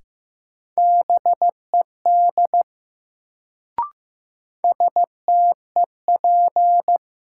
Morse code, heard as a keyed tone, about 15 wpm, 700 Hz. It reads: BED STEP
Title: Sets of 5 Words Encore - New Word Drill - 15wpm